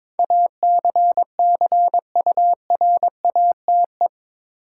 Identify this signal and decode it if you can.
Morse code, heard as a keyed tone, about 21 wpm, 700 Hz. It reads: ACCURATE
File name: practice / 22wpm / Eight Letter Words - 22wpm